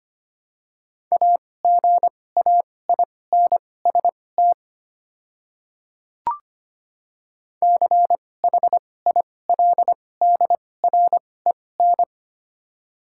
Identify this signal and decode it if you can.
Morse code, heard as a keyed tone, about 25 wpm, spaced as for 18 wpm, 700 Hz. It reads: AGAINST CHILDREN
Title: 2x Character Spacing - Top 500 Words - 25wpm